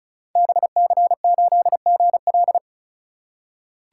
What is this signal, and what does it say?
Morse code, heard as a keyed tone, about 36 wpm, 700 Hz. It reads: BC8GL
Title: Code Groups - 5 Letter:Numbers - 35wpm